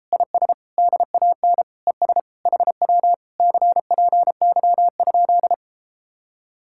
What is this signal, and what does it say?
Morse code, heard as a keyed tone, about 33 wpm, 700 Hz. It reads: IS DAN ES HW CPY?